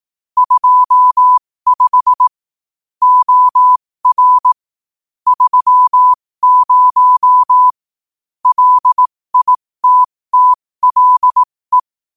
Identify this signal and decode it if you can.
Morse code, heard as a keyed tone, about 18 wpm, spaced as for 15 wpm, 1000 Hz. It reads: 25 OR 30 LITTLE
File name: qotdfiles / 15wpm